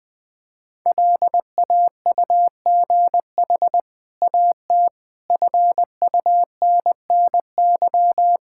Morse code, heard as a keyed tone, about 20 wpm, 700 Hz. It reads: LAUGH AT FUNNY
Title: Easy Sentences with Hints - 20wpm